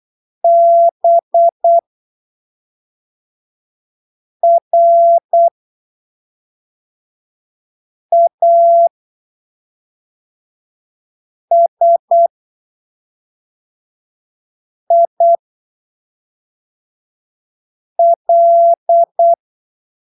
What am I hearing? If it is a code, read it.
Morse code, heard as a keyed tone, about 8 wpm, spaced as for 3 wpm, 680 Hz. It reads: BRASIL
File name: arquivos / texto / texto 14 08ppm